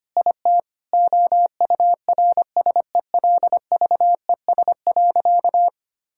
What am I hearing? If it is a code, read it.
Morse code, heard as a keyed tone, about 25 wpm, 700 Hz. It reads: IT OURSELVES.
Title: Sentences from Top 1600 Words 25wpm